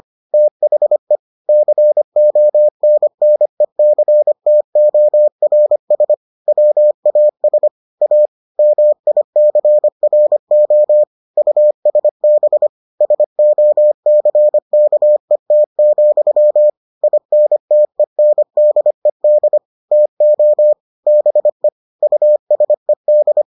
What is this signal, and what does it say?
Morse code, heard as a keyed tone, about 25 wpm, 600 Hz. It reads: THE CONNECTORS WAS A MICRO USB SOCKET, INTENDED TO BE USED